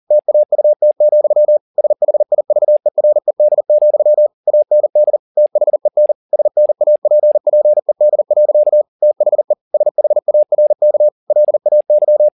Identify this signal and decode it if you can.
Morse code, heard as a keyed tone, about 40 wpm, 600 Hz. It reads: TAUT, SHIVERED, AND THEN SNAPPED. THE SHARK LAY
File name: the old man and the sea 2 40wpm